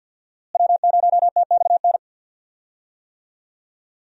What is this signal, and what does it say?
Morse code, heard as a keyed tone, about 51 wpm, 700 Hz. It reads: W0TXN
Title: USA Call Signs - 5 Characters - Repeated 3x - 50wpm